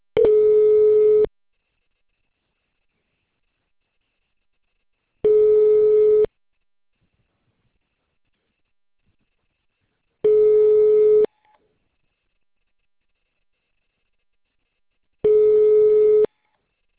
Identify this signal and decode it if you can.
Ringback tone (Europe, 425 Hz).